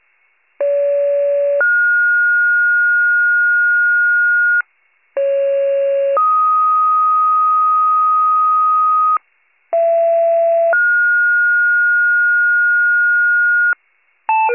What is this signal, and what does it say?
Two-tone page, heard as a one-second tone, then a three-second tone, about 570 then 1430 Hz, about 550 then 1250 Hz, about 670 then 1430 Hz, with noise. It is heard from 0.6 to 13.7 s.